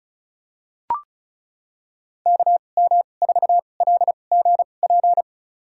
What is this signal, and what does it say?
Morse code, heard as a keyed tone, about 36 wpm, spaced as for 25 wpm, 700 Hz. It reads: KM4LGP
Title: USA Call Signs - 6 Characters - 2x Character Spacing - 35wpm